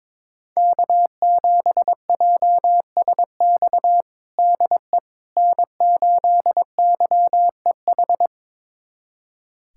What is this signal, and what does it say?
Morse code, heard as a keyed tone, about 22 wpm, 700 Hz. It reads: K7JSX DE N8YEH
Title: USA Callsign DE Callsign - 5 Characters - 22wpm